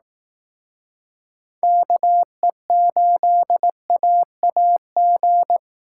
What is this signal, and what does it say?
Morse code, heard as a keyed tone, about 18 wpm, 700 Hz. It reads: KE8AAG
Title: Speed Racing - USA Call Signs 15wpm